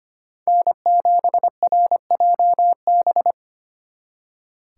Morse code, heard as a keyed tone, about 33 wpm, 700 Hz. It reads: N7RJB